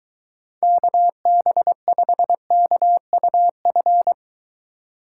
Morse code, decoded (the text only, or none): KB5KUF